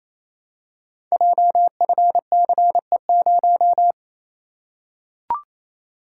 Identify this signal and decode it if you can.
Morse code, heard as a keyed tone, about 27 wpm, 700 Hz. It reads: JFCE0
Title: Code Groups - 5 Letter:Numbers - 28wpm